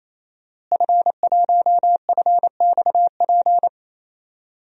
Morse code, heard as a keyed tone, about 28 wpm, 690 Hz. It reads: F1FXP
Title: DXCC-100 Call Signs - 5 Characters - Repeated 3x - 28wpm